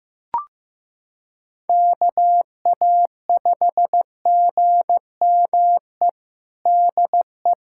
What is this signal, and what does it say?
Morse code, heard as a keyed tone, about 15 wpm, 700 Hz. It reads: KA5GME DE
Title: USA Callsign DE Callsign - 6 Characters - 15wpm